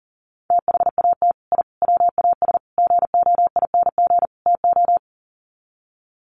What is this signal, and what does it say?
Morse code, heard as a keyed tone, about 40 wpm, 700 Hz. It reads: THAT I WAS GOING TO
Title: Sets of 5 Words Encore 40wpm